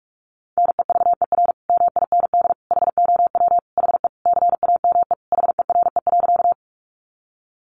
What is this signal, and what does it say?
Morse code, heard as a keyed tone, about 44 wpm, 700 Hz. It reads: NEVER MIND HOW HE CAME HERE.
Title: Sentences from Top 500 Words 45wpm